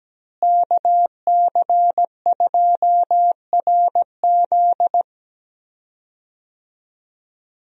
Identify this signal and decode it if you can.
Morse code, heard as a keyed tone, about 17 wpm, 700 Hz. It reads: KC2RZ
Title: USA Call Signs - 5 Characters - Repeated 2x - 17wpm